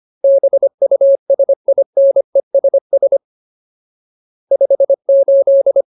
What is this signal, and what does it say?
Morse code, heard as a keyed tone, about 25 wpm, 550 Hz. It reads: BUSINESS 58
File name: Ultimate Word List / 25WPM Ultimate